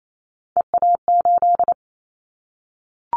Morse code, heard as a keyed tone, about 28 wpm, 700 Hz. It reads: EA8
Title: Callsign Prefixes for Top 100 DXCC - 28wpm